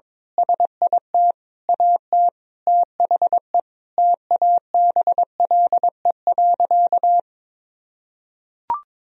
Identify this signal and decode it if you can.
Morse code, heard as a keyed tone, about 22 wpm, 700 Hz. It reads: SIT AT THE TABLE.
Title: Sentences from Top 1000 Words - Review Part 2 - 22wpm